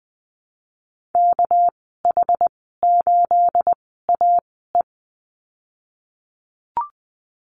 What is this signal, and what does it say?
Morse code, heard as a keyed tone, about 20 wpm, spaced as for 15 wpm, 700 Hz. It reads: KH8AE
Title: 219 Lesson 35g - Number 0 - Call Signs 20wpm